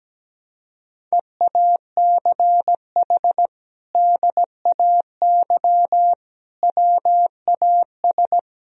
Morse code, heard as a keyed tone, about 17 wpm, 700 Hz. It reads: EACH DAY WAS
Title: Sentences from Top 1000 Words - Review Part 1 - 17wpm